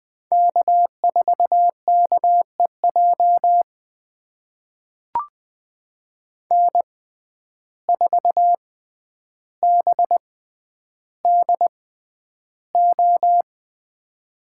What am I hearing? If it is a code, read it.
Morse code, heard as a keyed tone, about 20 wpm, spaced as for 10 wpm, 700 Hz. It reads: K4KEJ N4BDO
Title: USA Call Signs - 5 Characters - 6x Character Spacing - 20wpm